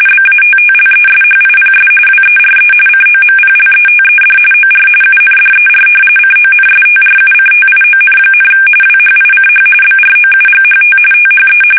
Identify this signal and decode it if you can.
RTTY, about 45.45 baud. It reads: BATTLESHIPS WERE COMPLETED OR UNDER CONSTRUCTION, WHICH BROUGHT THE
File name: fsk850 / usnavy3